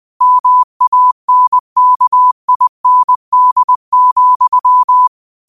Morse code, heard as a keyed tone, about 20 wpm, 1000 Hz. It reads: MANKIND,
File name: qotdfiles / 20wpm